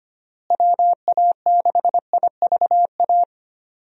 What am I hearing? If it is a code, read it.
Morse code, heard as a keyed tone, about 25 wpm, 700 Hz. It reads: WA6IVA
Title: USA Call Signs - 6 Characters - Repeated 3x - 25wpm